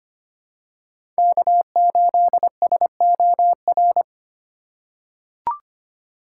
Morse code, heard as a keyed tone, about 25 wpm, 700 Hz. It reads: K8SOR
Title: USA Call Signs - 5 Characters - 4x Character Spacing - 25wpm